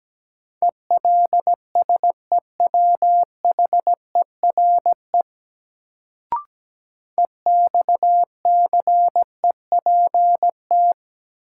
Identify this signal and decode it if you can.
Morse code, heard as a keyed tone, about 17 wpm, 700 Hz. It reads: ELSEWHERE EXCEPT
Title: Sentences from Top 1600 Words - New Word Drill - 17wpm